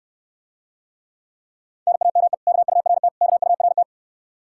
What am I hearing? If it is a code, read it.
Morse code, heard as a keyed tone, about 68 wpm, 700 Hz. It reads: NAME KURT KURT